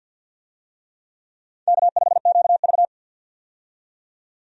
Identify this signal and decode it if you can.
Morse code, heard as a keyed tone, about 52 wpm, 700 Hz. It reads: K5QV
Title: USA Call Signs - 4 Characters - 50wpm